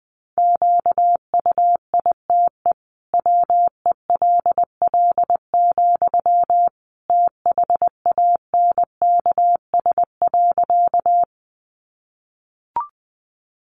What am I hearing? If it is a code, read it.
Morse code, heard as a keyed tone, about 20 wpm, 700 Hz. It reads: QUITE WELL, THANKS.
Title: Sentences from Top 700 Words 20wpm